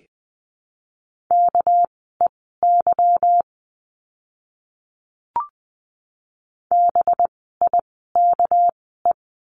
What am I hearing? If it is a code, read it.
Morse code, heard as a keyed tone, about 20 wpm, spaced as for 14 wpm, 700 Hz. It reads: KEY BIKE
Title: 199 Lesson 32f - Letter k - Words 20wpm